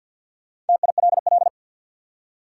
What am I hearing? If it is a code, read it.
Morse code, heard as a keyed tone, about 51 wpm, 700 Hz. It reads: TILL
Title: Four Letter Words - 50wpm